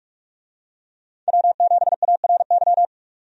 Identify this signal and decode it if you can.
Morse code, heard as a keyed tone, about 46 wpm, 700 Hz. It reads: W7ARY
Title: USA Call Signs - 5 Characters - 45wpm